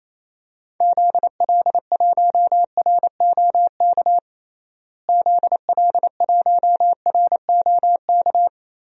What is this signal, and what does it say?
Morse code, heard as a keyed tone, about 28 wpm, 700 Hz. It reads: ZL1ROK ZL1ROK
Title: DXCC-100 Call Signs - 6 Characters - Repeated 3x - 28wpm